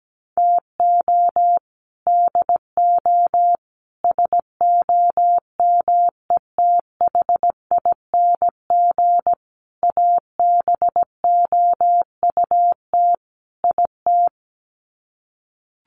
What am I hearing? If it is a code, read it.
Morse code, heard as a keyed tone, about 17 wpm, 700 Hz. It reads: TO DO SOMETHING ABOUT IT
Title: Sets of 5 Words 17wpm